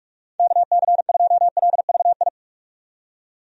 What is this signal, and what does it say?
Morse code, heard as a keyed tone, about 46 wpm, 700 Hz. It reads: KC2LVI
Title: USA Call Signs - 6 Characters - 45wpm